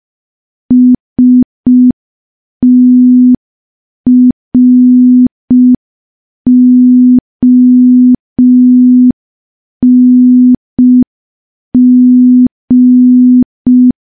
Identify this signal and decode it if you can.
Morse code, heard as a keyed tone, about 5 wpm, 250 Hz. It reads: STRONG
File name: morse / morse-training-05WPM